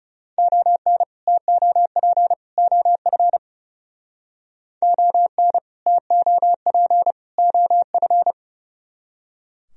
Morse code, heard as a keyed tone, about 37 wpm, 700 Hz. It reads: ON TOP OF ON TOP OF
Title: Speed-Racing - Sets of 3 Words 30wpm